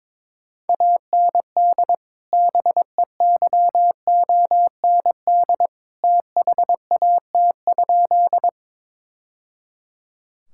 Morse code, heard as a keyed tone, about 22 wpm, 700 Hz. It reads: AND BEYOND THAT?
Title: Sentences from Top 1000 Words - Review Part 6 - 22wpm